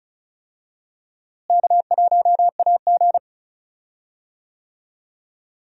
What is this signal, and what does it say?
Morse code, heard as a keyed tone, about 34 wpm, 700 Hz. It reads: K1AG
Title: USA Call Signs - 4 Characters - Repeated 3x - 35wpm